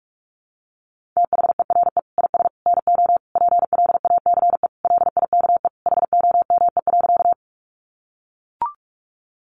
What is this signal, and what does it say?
Morse code, heard as a keyed tone, about 44 wpm, 700 Hz. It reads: THERE IS NO PLACE LIKE HOME.
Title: Sentences from Top 200 Words 45wpm